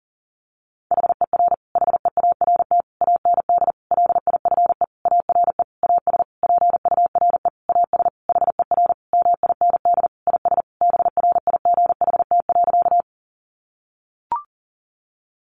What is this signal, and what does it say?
Morse code, heard as a keyed tone, about 40 wpm, 700 Hz. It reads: HER HEART AND LIFE ARE AS PURE AS HER MIND IS BRIGHT.
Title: Sentences from Top 1500 Words 40wpm